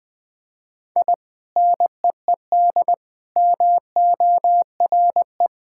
Morse code, heard as a keyed tone, about 20 wpm, 700 Hz. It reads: I NEED MORE
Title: Sentences with 2-Word Phrases - 20wpm